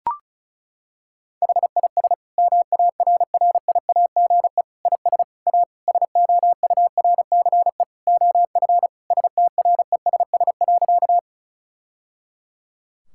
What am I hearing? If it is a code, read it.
Morse code, heard as a keyed tone, about 36 wpm, 700 Hz. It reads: HIS MARRIAGE IS A SOURCE OF STRESS.
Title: Sentences from Top 1000 Words - Review Part 6 - 35wpm